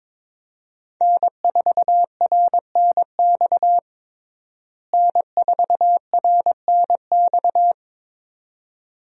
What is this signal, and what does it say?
Morse code, heard as a keyed tone, about 22 wpm, 700 Hz. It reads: N4RNX N4RNX
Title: USA Call Signs - 5 Characters - Repeated 2x - 22wpm